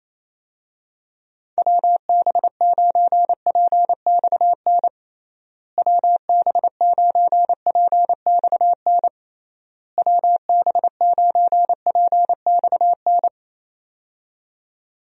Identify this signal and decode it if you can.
Morse code, heard as a keyed tone, about 28 wpm, 700 Hz. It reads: WB9PXN WB9PXN WB9PXN
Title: USA Call Signs - 6 Characters - Repeated 3x - 28wpm